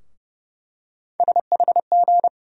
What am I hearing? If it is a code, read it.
Morse code, heard as a keyed tone, about 30 wpm, 700 Hz. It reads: SHG
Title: USA Call Sign Postfixes - 3 Letters 30wpm